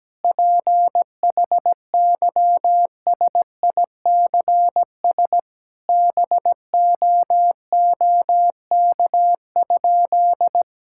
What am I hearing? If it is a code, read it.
Morse code, heard as a keyed tone, about 17 wpm, 700 Hz. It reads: PHYSICS BOOK?
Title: Sentences from Top 1000 Words - Review Part 5 - 17wpm